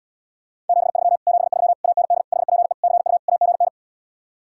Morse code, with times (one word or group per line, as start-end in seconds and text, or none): 0.7-1.2 73
1.3-1.7 73
1.8-2.2 AND
2.3-2.7 HPE
2.8-3.2 CU
3.3-3.7 AGN